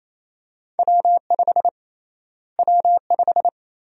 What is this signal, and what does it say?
Morse code, heard as a keyed tone, about 28 wpm, 700 Hz. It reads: W5 W5
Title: USA Call Sign Prefixes - Repeated 2x - 28wpm